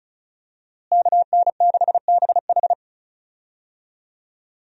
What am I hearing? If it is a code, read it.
Morse code, heard as a keyed tone, about 34 wpm, 700 Hz. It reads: KN6BH
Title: USA Call Signs - 5 Characters - Repeated 2x - 35wpm